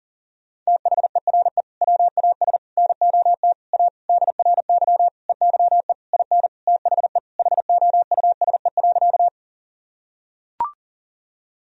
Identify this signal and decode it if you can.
Morse code, heard as a keyed tone, about 40 wpm, 700 Hz. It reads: THERE WAS NOT A DRY EYE IN THE HOUSE.
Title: Sentences from Top 1000 Words - Review Part 3 - 40wpm